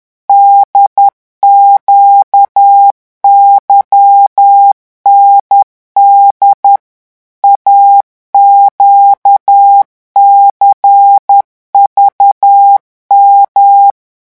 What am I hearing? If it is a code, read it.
Morse code, heard as a keyed tone, about 10 wpm, 800 Hz. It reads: DQYND AQCVM
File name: morse mp3 / revision3-vitesse 10 mots